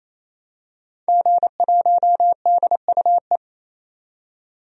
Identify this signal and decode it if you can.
Morse code, heard as a keyed tone, about 28 wpm, 700 Hz. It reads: G1DUE
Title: DXCC-100 Call Signs - 5 Characters - 28wpm